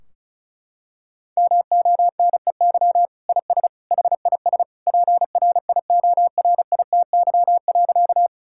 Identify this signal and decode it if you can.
Morse code, heard as a keyed tone, about 36 wpm, 700 Hz. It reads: MONEY IS HIS PRIORITY.
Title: Sentences from Top 1600 Words 35wpm